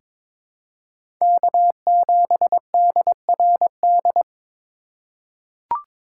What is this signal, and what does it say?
Morse code, heard as a keyed tone, about 22 wpm, 700 Hz. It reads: K7DRD
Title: USA Call Signs - 5 Characters - 3x Character Spacing - 22wpm